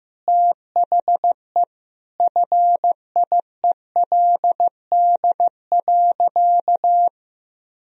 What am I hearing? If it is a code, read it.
Morse code, heard as a keyed tone, about 15 wpm, 700 Hz. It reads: THE FIELD.
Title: Sentences from Top 1000 Words - Review Part 4 - 15wpm